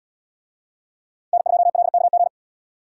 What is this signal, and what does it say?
Morse code, heard as a keyed tone, about 77 wpm, 700 Hz. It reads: N2LKG